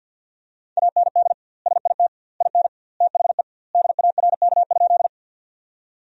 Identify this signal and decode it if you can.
Morse code, heard as a keyed tone, about 52 wpm, 700 Hz. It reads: AND SIT IN THE DARK?
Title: 2x Word-Spacing - Sentences from Top 500 Words - 50wpm